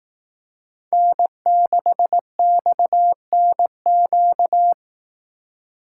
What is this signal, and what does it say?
Morse code, heard as a keyed tone, about 18 wpm, 700 Hz. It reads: N6XNQ
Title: Speed Racing - USA Call Signs 15wpm